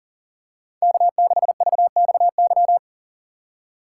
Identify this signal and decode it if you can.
Morse code, heard as a keyed tone, about 40 wpm, 700 Hz. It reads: K6VXY
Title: USA Call Signs - 5 Characters - 6x Character Spacing - 40wpm